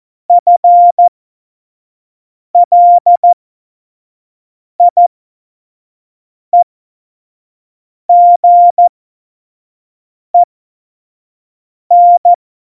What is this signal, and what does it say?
Morse code, heard as a keyed tone, about 14 wpm, spaced as for 5 wpm, 700 Hz. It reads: FLIEGEN
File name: Morsekurs / Witze / Witz 19. November 2019, 25 F70